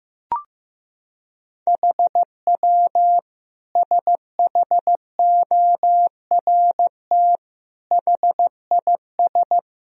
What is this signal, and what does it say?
Morse code, heard as a keyed tone, about 15 wpm, 700 Hz. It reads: HW SHORT HIS
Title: Sentences from Top 400 Words and Abbreviations 15wpm